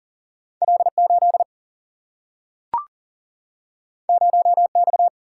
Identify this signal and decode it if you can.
Morse code, heard as a keyed tone, about 40 wpm, 700 Hz. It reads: L 8 0 X
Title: Single-Letter Single-Number 40wpm